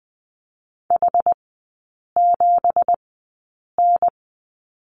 Morse code, heard as a keyed tone, about 20 wpm, spaced as for 8 wpm, 690 Hz. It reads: H7N
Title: 212 Lesson 34e - Number 8 - Three Characters 20wpm